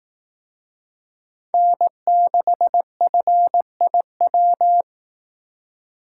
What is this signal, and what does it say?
Morse code, heard as a keyed tone, about 18 wpm, 700 Hz. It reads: N6FIW
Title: Speed Racing - USA 1x3 Call Signs 15wpm